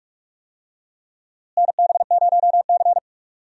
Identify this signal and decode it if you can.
Morse code, heard as a keyed tone, about 41 wpm, 700 Hz. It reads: NB0C
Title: USA Call Signs - 4 Characters - Repeated 3x - 45wpm